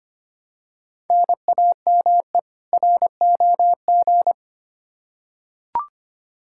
Morse code, heard as a keyed tone, about 25 wpm, 700 Hz. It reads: NAME ROG
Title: Practice QSO - Exchange 1 - Beginning - 25wpm